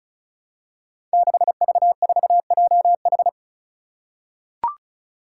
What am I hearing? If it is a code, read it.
Morse code, heard as a keyed tone, about 34 wpm, 700 Hz. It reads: 6V4JH
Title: Five Challenging Letter-Number Combinations 35wpm